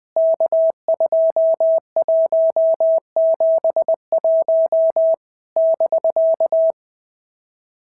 Morse code, heard as a keyed tone, about 20 wpm, 650 Hz. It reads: K2171 <BK>
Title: POTA QSO Practice - 20wpm